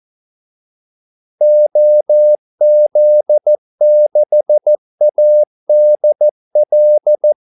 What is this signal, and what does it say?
Morse code, heard as a keyed tone, about 14 wpm, 600 Hz. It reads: OZ6ADL
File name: Callsigns-14wpm